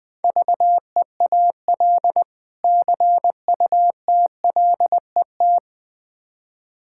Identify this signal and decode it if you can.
Morse code, heard as a keyed tone, about 20 wpm, 700 Hz. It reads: VEAL CUTLET
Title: Difficult Words with Hints - 20wpm